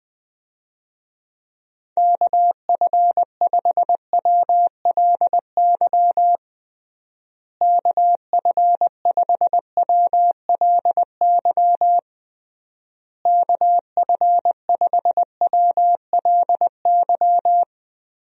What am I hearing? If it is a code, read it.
Morse code, heard as a keyed tone, about 20 wpm, 700 Hz. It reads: KF5WLY KF5WLY KF5WLY
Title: USA Call Signs - 6 Characters - Repeated 3x - 20wpm